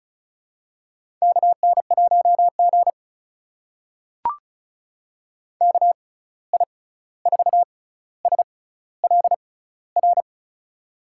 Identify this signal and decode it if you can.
Morse code, heard as a keyed tone, about 36 wpm, spaced as for 16 wpm, 700 Hz. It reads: KN1G KI4SLR
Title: USA Call Signs - 6x Character Spacing - 35wpm